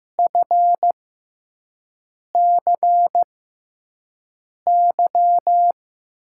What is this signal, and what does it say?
Morse code, heard as a keyed tone, about 15 wpm, spaced as for 5 wpm, 700 Hz. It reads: FCY
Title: USA Call Signs - 6 Characters - 6x Character Spacing - 15wpm